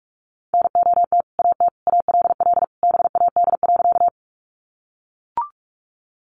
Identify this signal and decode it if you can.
Morse code, heard as a keyed tone, about 44 wpm, 700 Hz. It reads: NOT AT ALL BAD.
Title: Sentences from Top 700 Words 45wpm